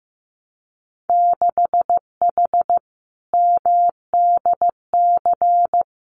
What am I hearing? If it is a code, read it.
Morse code, heard as a keyed tone, about 15 wpm, 700 Hz. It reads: 6H MDC
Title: Winter Field Day Exchanges Encore 15wpm